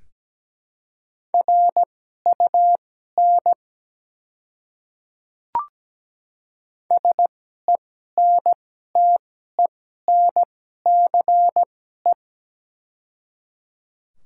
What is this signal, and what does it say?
Morse code, heard as a keyed tone, about 17 wpm, spaced as for 12 wpm, 700 Hz. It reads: RUN SENTENCE
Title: 2x Character Spacing - Top 300 Words - 17wpm